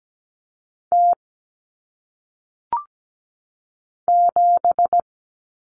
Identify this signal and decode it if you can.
Morse code, heard as a keyed tone, about 17 wpm, 700 Hz. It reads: T 7
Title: Common Cut Numbers - 1-Digit 17wpm